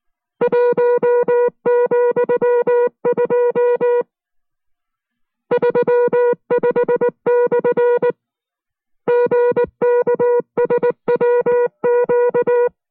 Morse code, heard as a keyed tone, about 19 wpm, spaced as for 15 wpm, 480 Hz. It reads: 1,235/ GKSWQ
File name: FreeFiles / CW-practice / 15WPM-side1